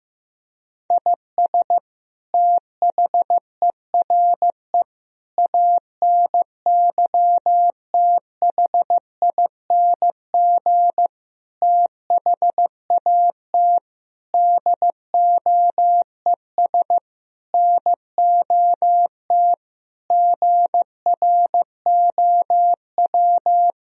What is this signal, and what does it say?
Morse code, heard as a keyed tone, about 16 wpm, 700 Hz. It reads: IS THERE ANYTHING THAT DOES NOT GROW
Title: Sentences from Top 1000 Words - Review Part 1 - 15wpm